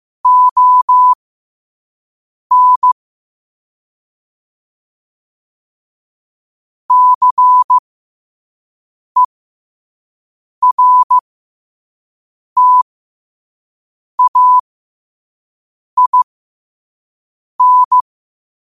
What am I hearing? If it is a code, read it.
Morse code, heard as a keyed tone, about 15 wpm, spaced as for 5 wpm, 1000 Hz. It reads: ON CERTAIN
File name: qotdfiles / 5wpm